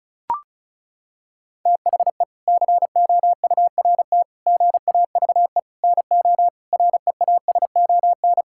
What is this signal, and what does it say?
Morse code, heard as a keyed tone, about 36 wpm, 700 Hz. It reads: THE COURT GAVE NO REASON
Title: Sentences from Top 1600 Words 35wpm